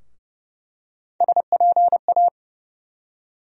Morse code, heard as a keyed tone, about 30 wpm, 700 Hz. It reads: SPA
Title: USA Call Sign Postfixes - 3 Letters 30wpm